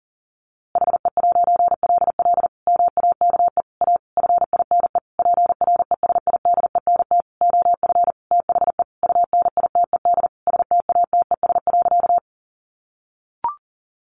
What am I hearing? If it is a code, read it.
Morse code, heard as a keyed tone, about 40 wpm, 700 Hz. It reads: HE'LL MAKE A FINE PRESIDENT OF THE UNITED STATES.
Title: A Sound Of Thunder 40wpm